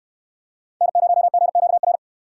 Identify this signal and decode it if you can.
Morse code, heard as a keyed tone, about 62 wpm, 700 Hz. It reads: N0GQR